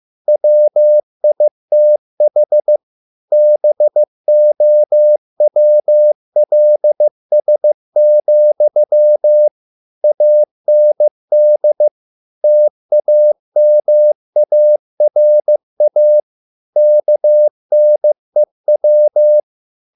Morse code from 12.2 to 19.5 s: TAMARA KNEW

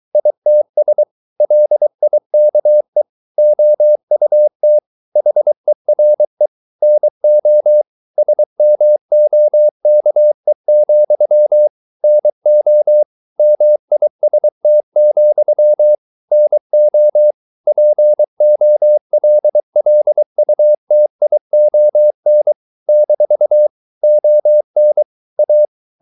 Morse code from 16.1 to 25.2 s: NO POLLUTION - ON

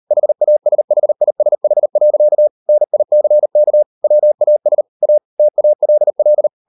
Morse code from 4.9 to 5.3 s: A

